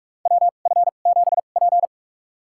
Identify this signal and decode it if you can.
Morse code, heard as a keyed tone, about 43 wpm, spaced as for 32 wpm, 700 Hz. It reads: WF7P
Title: USA Call Signs - 4 Characters - 2x Character Spacing - 45wpm